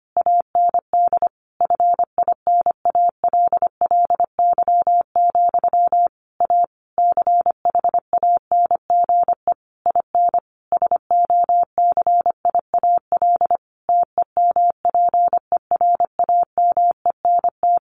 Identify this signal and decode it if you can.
Morse code, heard as a keyed tone, about 25 wpm, 700 Hz. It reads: AND FINALLY, A CHANGE IN SOCIAL TEMPERAMENT
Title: A Sound Of Thunder 25wpm